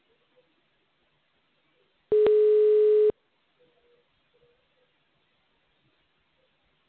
Ringback tone (Europe, 425 Hz).